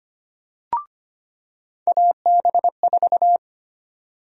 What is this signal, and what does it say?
Morse code, heard as a keyed tone, about 25 wpm, 700 Hz. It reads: AB4
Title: USA Call Sign Prefixes - 25wpm